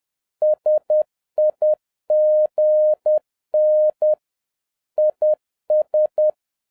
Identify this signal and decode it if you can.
Morse code, heard as a keyed tone, about 10 wpm, 610 Hz. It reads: SIGN IS